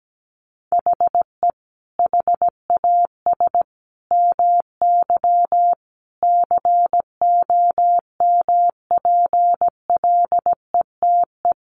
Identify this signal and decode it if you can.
Morse code, heard as a keyed tone, about 17 wpm, 700 Hz. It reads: HE HAS MY COMPLETE
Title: Sentences from Top 1500 Words 17wpm